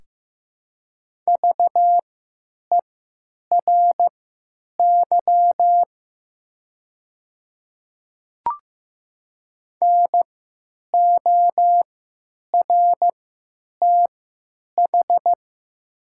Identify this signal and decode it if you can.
Morse code, heard as a keyed tone, about 15 wpm, spaced as for 9 wpm, 700 Hz. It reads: VERY NORTH
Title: 3x Character Spacing - Top 300 Words - 15wpm